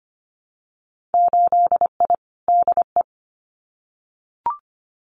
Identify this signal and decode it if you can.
Morse code, heard as a keyed tone, about 25 wpm, 700 Hz. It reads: 8I DE
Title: Winter Field Day Exchanges Encore 25wpm